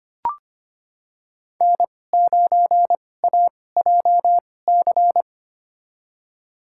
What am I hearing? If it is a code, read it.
Morse code, heard as a keyed tone, about 25 wpm, spaced as for 18 wpm, 700 Hz. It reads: N9AJC
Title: USA Call Signs - 5 Characters - 2x Character Spacing - 25wpm